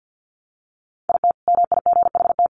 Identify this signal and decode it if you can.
Morse code, heard as a keyed tone, about 50 wpm, 700 Hz. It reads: IT MIGHT
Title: Sentences from Top 500 Words 50wpm